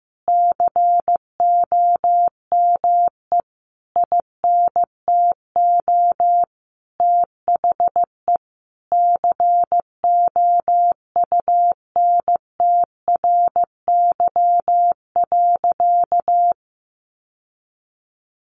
Morse code, heard as a keyed tone, about 15 wpm, 700 Hz. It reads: COME INTO THE COUNTRY.
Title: Sentences from Top 1200 Words 15wpm